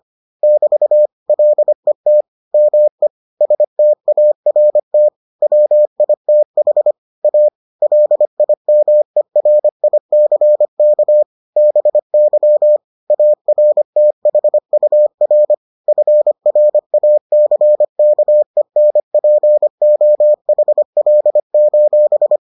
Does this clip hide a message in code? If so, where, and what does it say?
Yes: Morse code from 0.4 to 22.4 s, heard as a keyed tone, about 25 wpm, 600 Hz. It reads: <BT> LET ME START WITH A LIMERICK BY ARTHUR FRACKENPOHL: